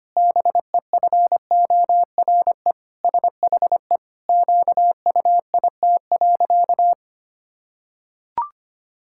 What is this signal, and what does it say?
Morse code, heard as a keyed tone, about 25 wpm, 700 Hz. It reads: BEFORE SHE QUIT.
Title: Pangram Challenge v2 - 25wpm